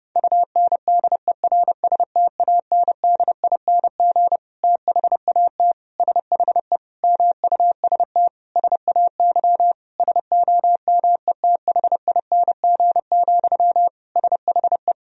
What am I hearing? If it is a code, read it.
Morse code, heard as a keyed tone, about 32 wpm, 700 Hz. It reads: UNDERSTANDING THAT SHE MUST SAY SOMETHING, SHE
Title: Sentences from Top 1000 Words - Review Part 3 - 30wpm